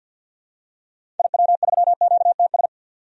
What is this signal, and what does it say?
Morse code, heard as a keyed tone, about 49 wpm, 700 Hz. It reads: IW3QTS
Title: DXCC-100 Call Signs - 6 Characters - 50wpm